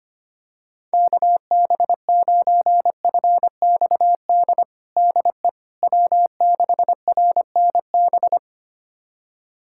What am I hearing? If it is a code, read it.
Morse code, heard as a keyed tone, about 25 wpm, 700 Hz. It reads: KB9FXD DE W6RNB
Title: USA Callsign DE Callsign 25wpm